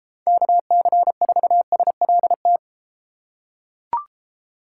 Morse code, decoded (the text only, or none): KC4SLT